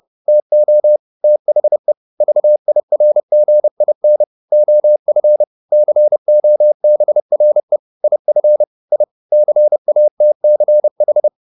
Morse code, heard as a keyed tone, about 30 wpm, 600 Hz. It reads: TO THE VIRGIN OF COBRE IF I CATCH
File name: the old man and the sea / the old man and the sea 1 30wpm